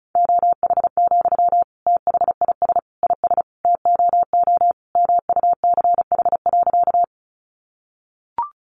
Morse code, heard as a keyed tone, about 34 wpm, 700 Hz. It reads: OH, THIS IS TOO MUCH.
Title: Sentences from Top 500 Words 35wpm